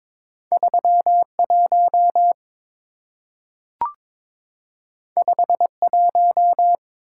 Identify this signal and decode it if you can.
Morse code, heard as a keyed tone, about 22 wpm, 700 Hz. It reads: 31 51
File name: practice / 22wpm / Two Letter-Number Combinations - 22wpm